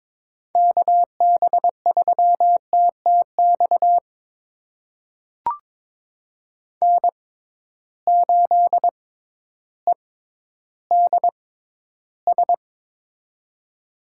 Morse code, heard as a keyed tone, about 22 wpm, spaced as for 12 wpm, 700 Hz. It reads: KB3TTX N8EDS
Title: USA Call Signs - 6x Character Spacing - 22wpm